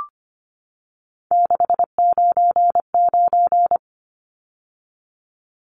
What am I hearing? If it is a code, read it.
Morse code, heard as a keyed tone, about 25 wpm, 700 Hz. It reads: 699
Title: Numbers - 3-Digit 25wpm